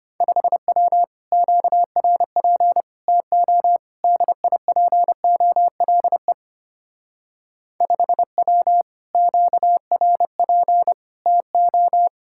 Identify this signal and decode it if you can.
Morse code, heard as a keyed tone, about 32 wpm, 700 Hz. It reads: 5W QRP TO DIPOLE 5W QRP TO